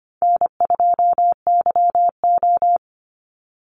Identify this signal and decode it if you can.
Morse code, heard as a keyed tone, about 25 wpm, 700 Hz. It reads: N2YO
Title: Easy Call Signs - 25wpm